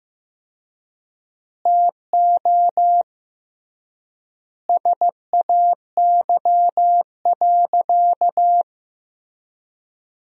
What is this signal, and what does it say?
Morse code, heard as a keyed tone, about 15 wpm, 700 Hz. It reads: TO SAY.
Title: 3x Word-Spacing - Sentences from Top 500 Words - 15wpm